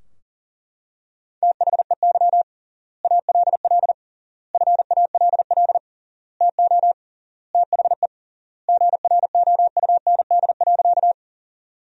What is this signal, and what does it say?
Morse code, heard as a keyed tone, about 37 wpm, 700 Hz. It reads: THEY ALL FALL TO THE GROUND.
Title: 3x Word-Spacing - Sentences from Top 500 Words - 40wpm